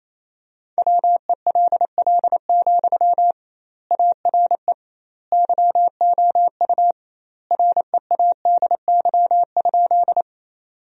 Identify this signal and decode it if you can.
Morse code, heard as a keyed tone, about 28 wpm, 700 Hz. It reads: WELL, ARE YOU READY?
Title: 2x Word-Spacing - Sentences from Top 400 Words - 28wpm